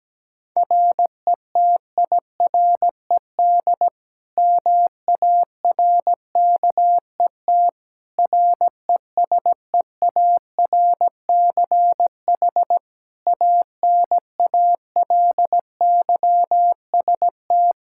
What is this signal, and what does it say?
Morse code, heard as a keyed tone, about 17 wpm, 700 Hz. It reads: RETIRED MARKET RESEARCH ANALYST
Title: Practice QSO - Exchange 3 - More Info - 17wpm